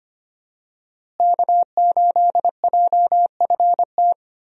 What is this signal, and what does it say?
Morse code, heard as a keyed tone, about 25 wpm, 700 Hz. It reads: K8JFT
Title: USA Call Signs - 5 Characters - Repeated 2x - 25wpm